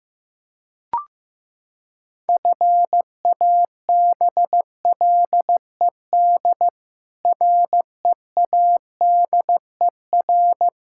Morse code, heard as a keyed tone, about 15 wpm, 700 Hz. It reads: FABLED READER
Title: Pangram Challenge v2 - 15wpm